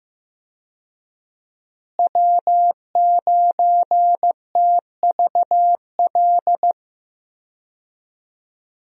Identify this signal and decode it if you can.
Morse code, heard as a keyed tone, about 15 wpm, 700 Hz. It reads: W9TVL
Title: USA Call Signs - 5 Characters - Repeated 2x - 15wpm